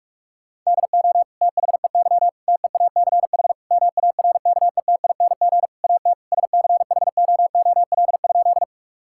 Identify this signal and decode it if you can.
Morse code, heard as a keyed tone, about 47 wpm, 700 Hz. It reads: DO THEY TEACH MARKETING AT SCHOOL?
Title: Sentences from Top 1000 Words - Review Part 5 - 45wpm